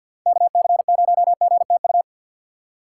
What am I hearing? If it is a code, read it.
Morse code, heard as a keyed tone, about 51 wpm, 700 Hz. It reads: KC0GTU
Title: USA Call Signs - 6 Characters - 6x Character Spacing - 50wpm